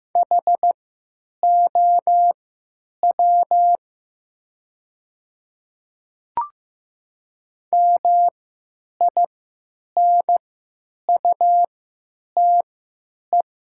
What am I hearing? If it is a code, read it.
Morse code, heard as a keyed tone, about 15 wpm, spaced as for 9 wpm, 700 Hz. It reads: HOW MINUTE
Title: 3x Character Spacing - Top 500 Words - 15wpm